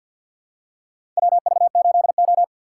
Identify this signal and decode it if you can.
Morse code, heard as a keyed tone, about 49 wpm, 700 Hz. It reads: WV8O
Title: USA Call Signs - 3x Character Spacing - 50wpm